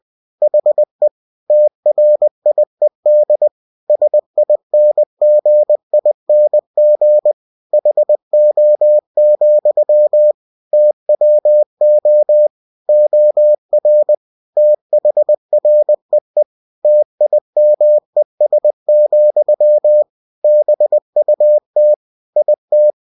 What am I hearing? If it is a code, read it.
Morse code, heard as a keyed tone, about 20 wpm, 600 Hz. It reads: HE TRIED SINGING HO, TWO OR THREE TIMES, BUT IT
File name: cw-books / winnie-the-pooh / winnie-20wpm-page-0075